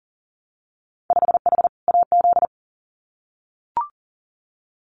Morse code, heard as a keyed tone, about 38 wpm, 700 Hz. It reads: 5H AZ
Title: Winter Field Day Exchanges Encore 40wpm